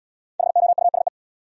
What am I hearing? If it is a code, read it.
Morse code, heard as a keyed tone, about 65 wpm, 700 Hz. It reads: HOUSE